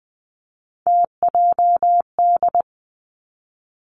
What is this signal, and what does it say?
Morse code, heard as a keyed tone, about 20 wpm, 700 Hz. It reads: TJD
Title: Three Challenging Letter-Number Combinations 20wpm